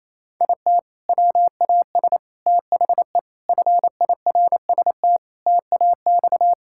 Morse code, heard as a keyed tone, about 28 wpm, 700 Hz. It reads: IT WAS THE FIRST TAX
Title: Sentences from Top 1000 Words - Review Part 5 - 28wpm